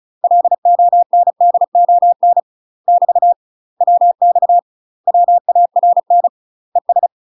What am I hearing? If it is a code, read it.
Morse code, heard as a keyed tone, about 34 wpm, 700 Hz. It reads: LONDON <BT> WX WARN ES
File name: UK MASTER QSOs TRAINING / 35WPM / UK MASTER QSOs 0024